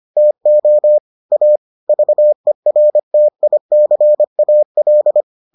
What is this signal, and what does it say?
Morse code, heard as a keyed tone, about 25 wpm, 600 Hz. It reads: TO A VERTICAL